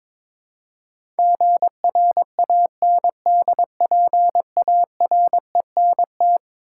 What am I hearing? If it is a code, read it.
Morse code, heard as a keyed tone, about 22 wpm, 700 Hz. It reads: GRANDPARENT
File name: practice / 22wpm / Eleven Letter Words - 22wpm